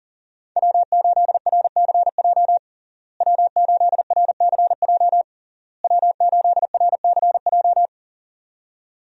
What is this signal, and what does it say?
Morse code, heard as a keyed tone, about 40 wpm, 700 Hz. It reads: W8RCJ W8RCJ W8RCJ
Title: USA Call Signs - 5 Characters - Repeated 3x - 40wpm